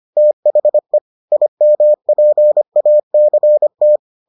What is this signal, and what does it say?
Morse code, heard as a keyed tone, about 25 wpm, 600 Hz. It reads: THE IMPACT